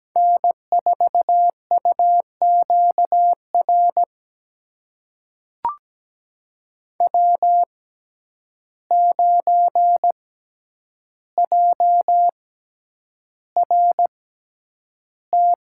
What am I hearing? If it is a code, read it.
Morse code, heard as a keyed tone, about 17 wpm, spaced as for 9 wpm, 700 Hz. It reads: N4UQR W9JRT
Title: USA Call Signs - 5 Characters - 6x Character Spacing - 17wpm